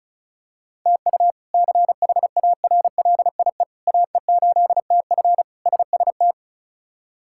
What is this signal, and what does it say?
Morse code, heard as a keyed tone, about 36 wpm, 700 Hz. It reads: TU CHARLIE AE8TF SST
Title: K1USN SST QSO Practice - 35wpm